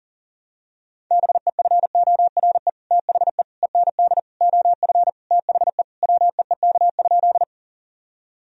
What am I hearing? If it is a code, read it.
Morse code, heard as a keyed tone, about 37 wpm, 700 Hz. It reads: BEFORE THE END OF THE WEEK?
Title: Sentences from Top 1000 Words - Review Part 2 - 40wpm